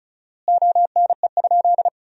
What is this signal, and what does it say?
Morse code, heard as a keyed tone, about 36 wpm, 700 Hz. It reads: ONE?
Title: 4x Word-Spacing - Sentences from Top 100 Words - 35wpm